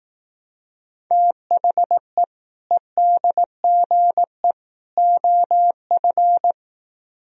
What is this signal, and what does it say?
Morse code, heard as a keyed tone, about 18 wpm, 700 Hz. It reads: THE EDGE OF
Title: Speed-Racing - Sets of 3 Words 15wpm